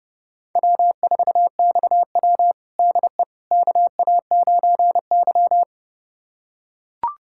Morse code, heard as a keyed tone, about 29 wpm, 700 Hz. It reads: W4XW DE KA9Y
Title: USA Callsign DE Callsign - 4 Characters - 30wpm